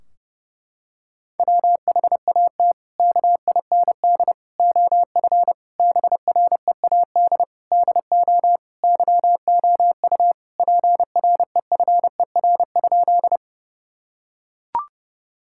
Morse code, heard as a keyed tone, about 30 wpm, 700 Hz. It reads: WHAT KIND OF BREAD DO YOU PREFER?
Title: Sentences from Top 1600 Words 30wpm